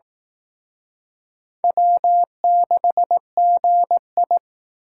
Morse code, heard as a keyed tone, about 19 wpm, 700 Hz. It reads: W6GI
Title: Speed Racing - USA 1x2 Call Signs 15wpm